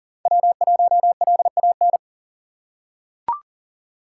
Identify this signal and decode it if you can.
Morse code, heard as a keyed tone, about 37 wpm, 700 Hz. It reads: W1LAN
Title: USA Call Signs - 5 Characters - 6x Character Spacing - 40wpm